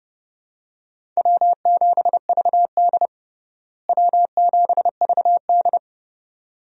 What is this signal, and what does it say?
Morse code, heard as a keyed tone, about 32 wpm, 700 Hz. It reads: W7VD W7VD
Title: USA Call Signs - 4 Characters - Repeated 2x - 30wpm